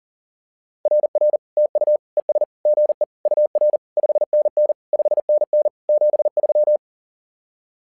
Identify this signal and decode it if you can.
Morse code, heard as a keyed tone, about 40 wpm, 600 Hz. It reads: RR TU ES GE UR 5NN 5NN 73
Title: SOTA QSO Practice 40wpm